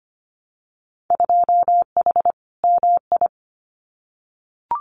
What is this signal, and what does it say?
Morse code, heard as a keyed tone, about 25 wpm, 700 Hz. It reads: 2H MI
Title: Winter Field Day Exchanges 25wpm